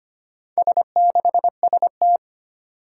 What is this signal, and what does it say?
Morse code, heard as a keyed tone, about 25 wpm, 700 Hz. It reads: S6ST
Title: Four Challenging Letter-Number Combinations 25wpm